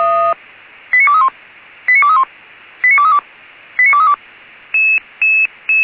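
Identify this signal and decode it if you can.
Five-tone selective call (ZVEI), heard as quick runs of five tones, tones of about 70 ms. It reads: 89231 89231 89232 89232
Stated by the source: Five-tone ZVEI selective call received by SDR.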